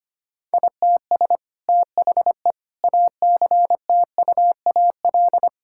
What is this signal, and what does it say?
Morse code, heard as a keyed tone, about 25 wpm, 700 Hz. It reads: ITS THE ACTUAL
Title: Sentences from Top 1000 Words - Review Part 5 - 25wpm